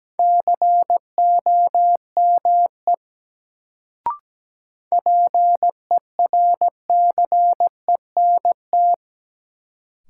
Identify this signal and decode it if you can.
Morse code, heard as a keyed tone, about 17 wpm, 700 Hz. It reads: COME PERCENT
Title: Sets of 2 Words - New Word Drill - 17wpm